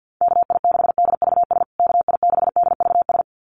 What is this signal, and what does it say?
Morse code, heard as a keyed tone, about 50 wpm, 700 Hz. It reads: KI6DVS KI6DVS
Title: USA Call Signs Repeated 2x 50wpm